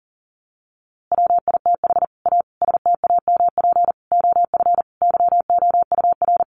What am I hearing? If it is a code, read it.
Morse code, heard as a keyed tone, about 40 wpm, 700 Hz. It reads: WITH A STAMP OF YOUR
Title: A Sound Of Thunder 40wpm